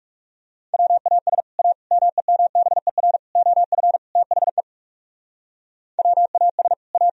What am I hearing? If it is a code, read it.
Morse code, heard as a keyed tone, about 37 wpm, 700 Hz. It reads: WAS A MEMBER OF THE WAS A
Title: Speed-Racing - Sets of 5 Words 30wpm